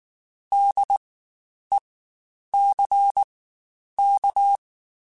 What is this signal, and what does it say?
Morse code, heard as a keyed tone, about 19 wpm, spaced as for 9 wpm, 800 Hz. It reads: DECK
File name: Spaced Alphas / Spaced Alphas mp3s / Ds sp @18